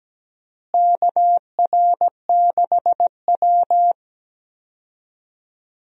Morse code, heard as a keyed tone, about 17 wpm, 700 Hz. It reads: KR6W
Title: USA Call Signs - 4 Characters - Repeated 2x - 17wpm